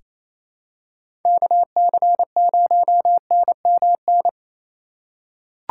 Morse code, heard as a keyed tone, about 28 wpm, 700 Hz. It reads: KC0NMN
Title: USA Call Signs - 6 Characters - 3x Character Spacing - 28wpm